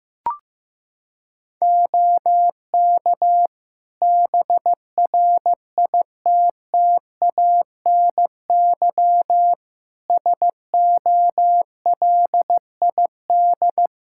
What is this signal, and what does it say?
Morse code, heard as a keyed tone, about 15 wpm, 700 Hz. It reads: OK BRITTANY SOLID
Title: K1USN SST QSO Practice 15wpm